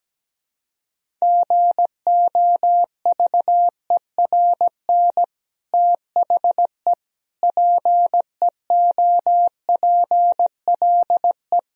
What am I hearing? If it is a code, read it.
Morse code, heard as a keyed tone, about 17 wpm, 700 Hz. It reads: GOVERN THE PEOPLE
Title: Sentences from Top 1000 Words - Review Part 2 - 17wpm